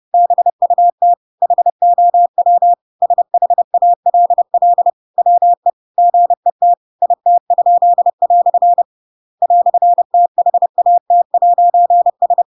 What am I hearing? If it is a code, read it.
Morse code, heard as a keyed tone, about 30 wpm, 700 Hz. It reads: BUT HOW SHALL WE GET IT?" "THAT'S
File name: Pinocchio / Pinocchio ch8 30 30 ik1hge